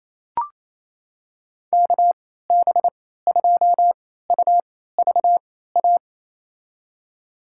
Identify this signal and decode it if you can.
Morse code, heard as a keyed tone, about 28 wpm, spaced as for 16 wpm, 700 Hz. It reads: KB2UVA
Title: USA Call Signs - 6 Characters - 3x Character Spacing - 28wpm